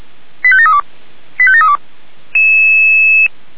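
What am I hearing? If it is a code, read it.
Five-tone selective call (ZVEI), heard as quick runs of five tones, tones of about 70 ms. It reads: 85732 85732
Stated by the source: ZVEI five-tone call received on an SDR.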